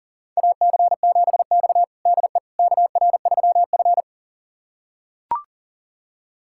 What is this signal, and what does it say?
Morse code, heard as a keyed tone, about 40 wpm, 700 Hz. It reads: AC7X DE KR3F
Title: USA Callsign DE Callsign - 4 Characters - 40wpm